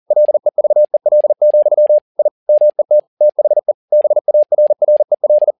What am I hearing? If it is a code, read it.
Morse code, heard as a keyed tone, about 40 wpm, 600 Hz. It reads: LEVEL, I MET THE BARREL
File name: learning center / tower trouble / tower touble 40wpm